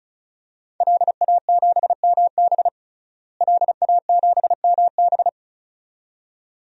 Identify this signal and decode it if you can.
Morse code, heard as a keyed tone, about 36 wpm, 700 Hz. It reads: LA7MB LA7MB
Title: DXCC-100 Call Signs - 5 Characters - Repeated 2x - 35wpm